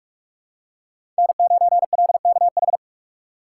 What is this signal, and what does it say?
Morse code, heard as a keyed tone, about 46 wpm, 700 Hz. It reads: N9LKH
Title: USA Call Signs - 5 Characters - 2x Character Spacing - 45wpm